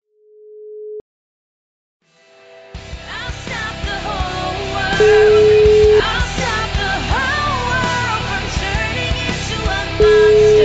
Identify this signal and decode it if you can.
Ringback tone (Europe, 425 Hz), with noise.